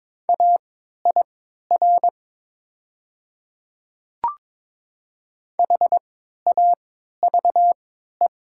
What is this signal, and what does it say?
Morse code, heard as a keyed tone, about 22 wpm, spaced as for 12 wpm, 700 Hz. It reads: AIR HAVE
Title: 3x Character Spacing - Top 500 Words - 22wpm